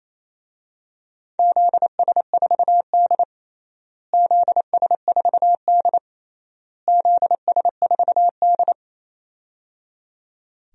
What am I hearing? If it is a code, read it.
Morse code, heard as a keyed tone, about 28 wpm, 700 Hz. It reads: ZS4D ZS4D ZS4D
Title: DXCC-100 Call Signs - 4 Characters - Repeated 3x - 28wpm